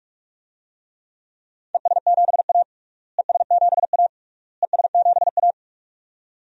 Morse code, heard as a keyed tone, about 46 wpm, 700 Hz. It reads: ES7A ES7A ES7A
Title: DXCC-100 Call Signs - 4 Characters - Repeated 3x - 45wpm